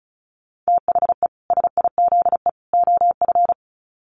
Morse code, heard as a keyed tone, about 34 wpm, 700 Hz. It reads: THE SIZE OF
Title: Sets of 3 Words 35wpm